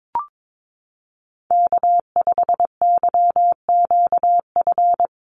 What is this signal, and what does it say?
Morse code, heard as a keyed tone, about 22 wpm, 700 Hz. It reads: K5YQF
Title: Call Signs - 22wpm